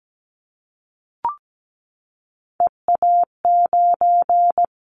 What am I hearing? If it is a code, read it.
Morse code, heard as a keyed tone, about 17 wpm, 700 Hz. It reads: EA9
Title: Callsign Prefixes for Top 100 DXCC - 17wpm